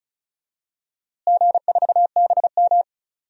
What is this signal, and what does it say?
Morse code, heard as a keyed tone, about 34 wpm, 700 Hz. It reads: G4BM
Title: DXCC-100 Call Signs - 4 Characters - Repeated 2x - 35wpm